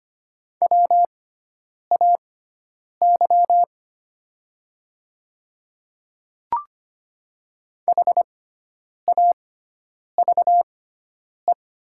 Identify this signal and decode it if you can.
Morse code, heard as a keyed tone, about 25 wpm, spaced as for 9 wpm, 700 Hz. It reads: WAY HAVE
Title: 6x Character Spacing - Top 200 Words - 25wpm